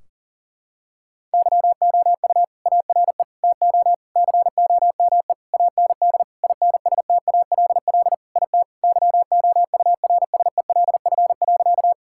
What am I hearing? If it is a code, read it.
Morse code, heard as a keyed tone, about 40 wpm, 700 Hz. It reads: YOU ARE TO COME AND INSTALL IT YOURSELF.
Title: Sentences from Top 1600 Words 40wpm